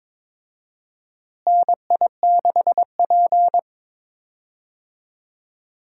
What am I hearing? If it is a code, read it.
Morse code, heard as a keyed tone, about 22 wpm, 700 Hz. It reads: NI6P
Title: USA Call Signs - 4 Characters - Repeated 2x - 22wpm